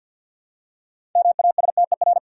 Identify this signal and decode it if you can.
Morse code, heard as a keyed tone, about 53 wpm, 700 Hz. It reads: MASTER
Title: Six Letter Words - 50wpm